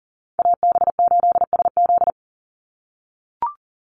Morse code, heard as a keyed tone, about 40 wpm, 700 Hz. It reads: AB8SZ
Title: USA Call Signs 40wpm